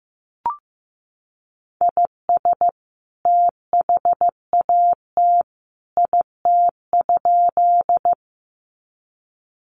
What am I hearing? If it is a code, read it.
Morse code, heard as a keyed tone, about 15 wpm, 700 Hz. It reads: IS THAT IT?
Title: Sentences from Top 100 Words 15wpm